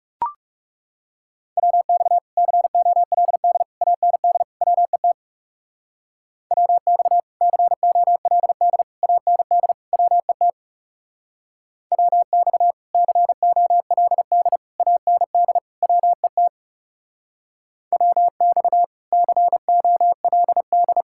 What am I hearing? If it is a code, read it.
Morse code, heard as a keyed tone, about 38 wpm, 700 Hz. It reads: WX COLD AND WET WX COLD AND WET WX COLD AND WET WX COLD